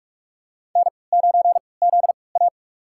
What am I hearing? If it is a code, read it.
Morse code, heard as a keyed tone, about 46 wpm, spaced as for 25 wpm, 700 Hz. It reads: N9ZA
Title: USA Call Signs - 3x Character Spacing - 45wpm